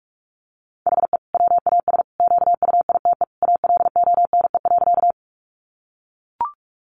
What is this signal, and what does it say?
Morse code, heard as a keyed tone, about 44 wpm, 700 Hz. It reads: HE WAS QUITE ALONE.
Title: Sentences from Top 1000 Words 45wpm